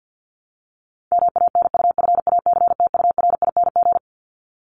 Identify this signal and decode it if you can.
Morse code, heard as a keyed tone, about 50 wpm, 700 Hz. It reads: MANUFACTURING
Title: ING Suffix - Top 100 Words with Base Words 50wpm